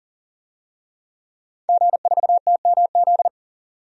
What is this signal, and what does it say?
Morse code, heard as a keyed tone, about 40 wpm, 700 Hz. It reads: G4TMZ
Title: DXCC-100 Call Signs - 5 Characters - 40wpm